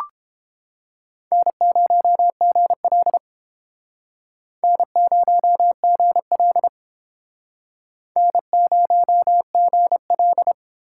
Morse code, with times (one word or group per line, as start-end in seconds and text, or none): 1.3-3.2 N0GL
4.6-6.7 N0GL
8.2-10.5 N0GL